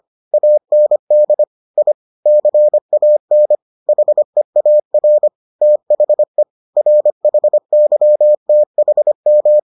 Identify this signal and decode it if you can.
Morse code, heard as a keyed tone, about 25 wpm, 600 Hz. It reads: AND I CAN HEAR THE RHYTHM